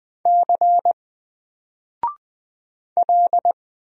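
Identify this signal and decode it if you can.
Morse code, heard as a keyed tone, about 20 wpm, 700 Hz. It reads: C L
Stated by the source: Single letter and number Morse drill at 20 wpm, from N4LCD.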